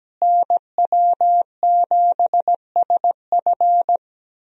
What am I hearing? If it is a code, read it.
Morse code, heard as a keyed tone, about 17 wpm, 700 Hz. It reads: NW7SF
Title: USA Call Signs - 5 Characters - Repeated 3x - 17wpm